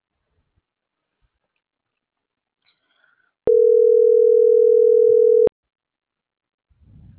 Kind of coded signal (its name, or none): telephone call progress tone